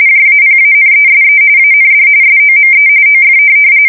RTTY, about 45.45 baud. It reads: JAPANESE AND SOME OF
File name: fsk170 / USS-Missouri